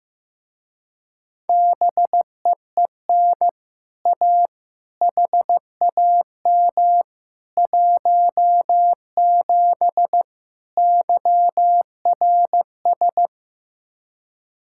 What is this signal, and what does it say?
Morse code, heard as a keyed tone, about 15 wpm, 700 Hz. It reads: BEEN A HAM 17 YRS
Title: Speed-Racing Common QSO Phrases Encore - 15wpm